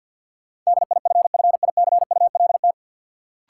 Morse code, heard as a keyed tone, about 48 wpm, 700 Hz. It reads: DIFFICULT
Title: Sentences from Top 600 Words - New Word Drill - 50wpm